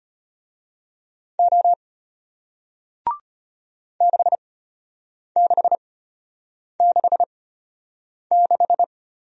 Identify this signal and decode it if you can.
Morse code, heard as a keyed tone, about 34 wpm, spaced as for 8 wpm, 700 Hz. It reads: O 6666